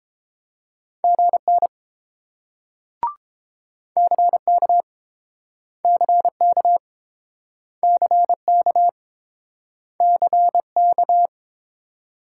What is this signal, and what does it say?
Morse code, heard as a keyed tone, about 32 wpm, 700 Hz. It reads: GN CK CK CK CK